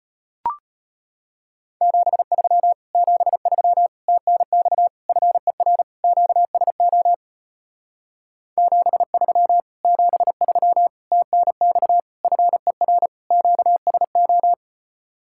Morse code, heard as a keyed tone, about 33 wpm, 700 Hz. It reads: 73 73 TNX FER QSO 73 73 TNX FER QSO